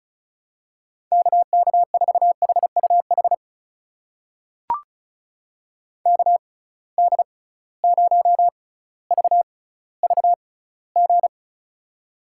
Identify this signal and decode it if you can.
Morse code, heard as a keyed tone, about 36 wpm, spaced as for 18 wpm, 700 Hz. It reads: KK4HUH KD0VVG
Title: USA Call Signs - 6 Characters - 6x Character Spacing - 35wpm